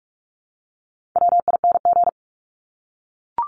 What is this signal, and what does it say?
Morse code, heard as a keyed tone, about 44 wpm, 700 Hz. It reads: WING
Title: Top 1001-1250 Words 45wpm